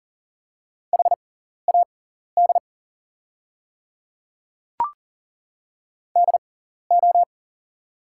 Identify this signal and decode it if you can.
Morse code, heard as a keyed tone, about 40 wpm, spaced as for 14 wpm, 700 Hz. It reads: HAD DO
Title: 6x Character Spacing - Top 100 Words - 40wpm